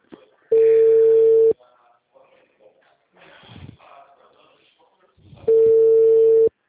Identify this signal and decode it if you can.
Ringback tone (China, 450 Hz), with speech.